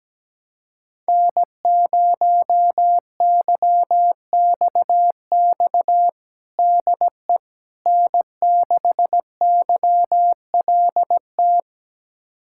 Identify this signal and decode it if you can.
Morse code, heard as a keyed tone, about 17 wpm, 700 Hz. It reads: N0YXX DE N6YLT
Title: USA Callsign DE Callsign - 5 Characters - 17wpm